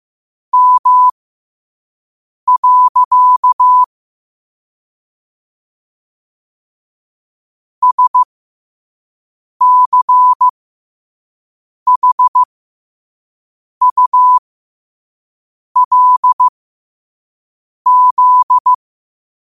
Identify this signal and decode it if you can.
Morse code, heard as a keyed tone, about 15 wpm, spaced as for 5 wpm, 1000 Hz. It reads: M. SCHULZ
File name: qotdfiles / 5wpm